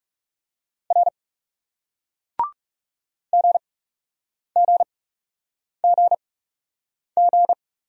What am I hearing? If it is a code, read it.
Morse code, heard as a keyed tone, about 41 wpm, spaced as for 8 wpm, 700 Hz. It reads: R GGGG